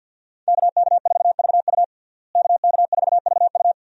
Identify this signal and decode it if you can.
Morse code, heard as a keyed tone, about 51 wpm, 700 Hz. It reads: KK4VU KK4VU
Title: USA Call Signs - 5 Characters - Repeated 2x - 50wpm